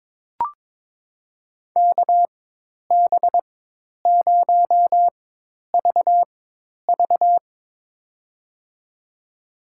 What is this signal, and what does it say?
Morse code, heard as a keyed tone, about 22 wpm, spaced as for 10 wpm, 700 Hz. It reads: KB0VV
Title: USA Call Signs - 5 Characters - 4x Character Spacing - 22wpm